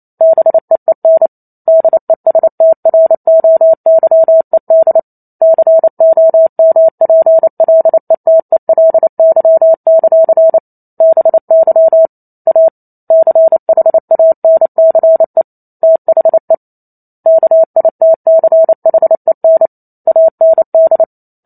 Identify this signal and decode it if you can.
Morse code, heard as a keyed tone, about 28 wpm, 650 Hz. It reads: BEEN DESTROYED COMPLETELY; BY A CHANCE THE KITCHEN AND
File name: Chapter 2 - 28 WPM